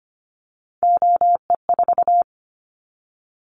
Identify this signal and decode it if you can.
Morse code, heard as a keyed tone, about 25 wpm, 700 Hz. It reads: OE4
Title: Callsign Prefixes for Top 50 DXCC - 25wpm